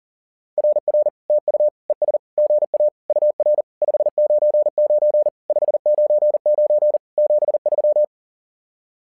Morse code, heard as a keyed tone, about 40 wpm, 600 Hz. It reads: RR TU ES GA UR 599 599 73
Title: SOTA QSO Practice 40wpm